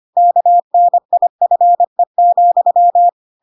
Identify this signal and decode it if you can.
Morse code, heard as a keyed tone, about 25 wpm, 700 Hz. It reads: KNIFE,
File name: Pinocchio / Pinocchio ch7 25 25 ik1hge